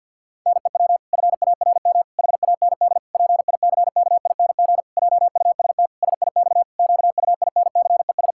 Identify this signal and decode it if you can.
Morse code, heard as a keyed tone, about 50 wpm, 700 Hz. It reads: NEW FARM HAND PICKING JUST SIX QUINCES
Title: Pangram Challenge v2 - 50wpm